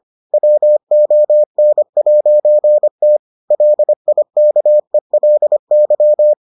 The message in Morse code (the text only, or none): WON'T LIKELY